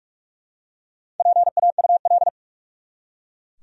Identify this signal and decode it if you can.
Morse code, heard as a keyed tone, about 42 wpm, 700 Hz. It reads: PAUL
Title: USA Common Names - Max 5 Letters - 45wpm